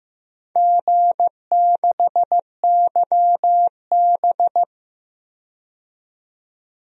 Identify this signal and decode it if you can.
Morse code, heard as a keyed tone, about 15 wpm, 700 Hz. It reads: G6YB
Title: DXCC-100 Call Signs - 4 Characters - 15wpm